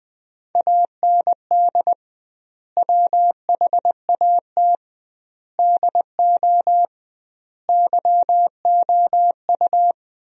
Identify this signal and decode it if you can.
Morse code, heard as a keyed tone, about 20 wpm, 700 Hz. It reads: AND WHAT DO YOU
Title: 2x Word-Spacing - Sentences from Top 200 Words - 20wpm